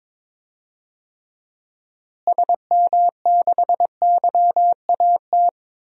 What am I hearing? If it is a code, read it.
Morse code, heard as a keyed tone, about 21 wpm, 700 Hz. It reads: SM6YAT
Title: DXCC-100 Call Signs - 6 Characters - Repeated 2x - 22wpm